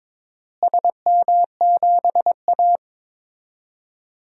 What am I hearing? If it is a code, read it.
Morse code, heard as a keyed tone, about 22 wpm, 700 Hz. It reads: SM7A
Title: DXCC-100 Call Signs - 4 Characters - Repeated 3x - 22wpm